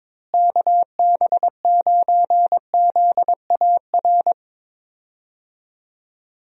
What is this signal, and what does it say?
Morse code, heard as a keyed tone, about 22 wpm, 700 Hz. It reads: KB9ZAR
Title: USA Call Signs - 6 Characters - Repeated 3x - 22wpm